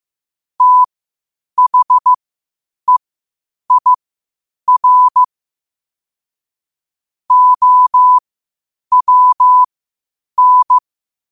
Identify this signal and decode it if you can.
Morse code, heard as a keyed tone, about 15 wpm, spaced as for 8 wpm, 1000 Hz. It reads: THEIR OWN
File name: qotdfiles / 8wpm